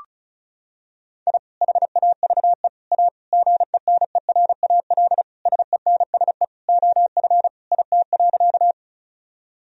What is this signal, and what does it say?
Morse code, heard as a keyed tone, about 36 wpm, 700 Hz. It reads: I HAVE A GENERAL SENSE OF IT.
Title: Sentences from Top 1000 Words - Review Part 4 - 35wpm